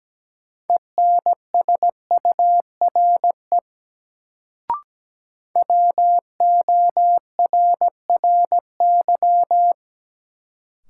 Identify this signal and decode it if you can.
Morse code, heard as a keyed tone, about 17 wpm, 700 Hz. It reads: ENSURE WORRY
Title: Sentences from Top 1100 Words - New Word Drill Alt - 17wpm